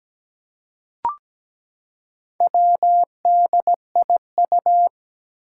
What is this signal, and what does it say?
Morse code, heard as a keyed tone, about 17 wpm, 700 Hz. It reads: WDIU
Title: Four Challenging Letter-Number Combinations 17wpm